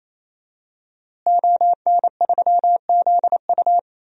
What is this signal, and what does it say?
Morse code, heard as a keyed tone, about 28 wpm, 700 Hz. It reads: ON3ZU
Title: DXCC-100 Call Signs - 5 Characters - Repeated 3x - 28wpm